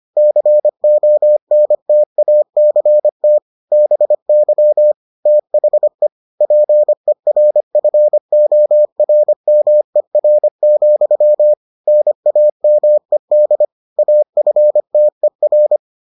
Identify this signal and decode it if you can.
Morse code, heard as a keyed tone, about 25 wpm, 600 Hz. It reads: CONTACT BY THE PERFORMER, NAMED AFTER